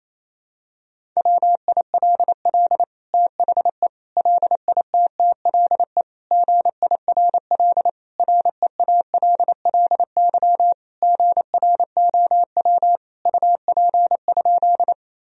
Morse code, heard as a keyed tone, about 28 wpm, 700 Hz. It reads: WILL THE LITTLE GIRL REALLY GROW UP?
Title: Sentences from Top 1000 Words - Review Part 1 - 28wpm